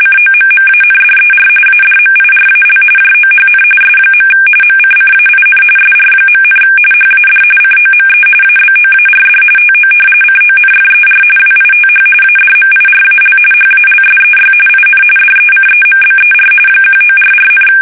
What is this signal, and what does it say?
RTTY, about 45.45 baud. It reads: COMMANDER ADMIRAL JOHN S. MCCAIN SR., HAD COME ABOARD TO CONFER WITH ADMIRAL HALSEY IN DECEMBER 1944,
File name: fsk850 / battleshipnj